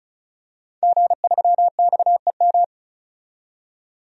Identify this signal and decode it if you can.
Morse code, heard as a keyed tone, about 36 wpm, 700 Hz. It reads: G3XEM
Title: DXCC-100 Call Signs - 5 Characters - 35wpm